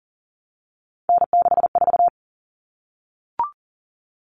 Morse code, heard as a keyed tone, about 40 wpm, 700 Hz. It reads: N64
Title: Common Cut Numbers - 3-Digit 40wpm